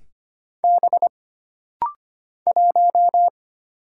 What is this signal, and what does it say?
Morse code, heard as a keyed tone, about 25 wpm, 700 Hz. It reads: B 1
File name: Single Character (Letter:Number:Punctuation:Prosign) - Rapid-Fire - 25wpm